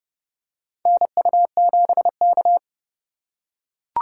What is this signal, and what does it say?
Morse code, heard as a keyed tone, about 30 wpm, 700 Hz. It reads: NU7K
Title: USA Call Signs - 4 Characters - 3x Character Spacing - 30wpm